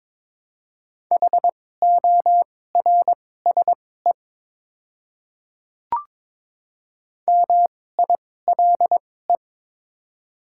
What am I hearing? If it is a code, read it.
Morse code, heard as a keyed tone, about 22 wpm, spaced as for 16 wpm, 700 Hz. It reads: HORSE MILE
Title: 2x Character Spacing - Top 300 Words - 22wpm